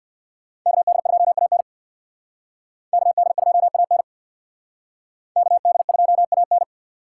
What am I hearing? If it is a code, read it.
Morse code, heard as a keyed tone, about 59 wpm, 700 Hz. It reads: KD2AN KD2AN KD2AN
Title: Speed Racing - USA 2x2 Call Signs 50wpm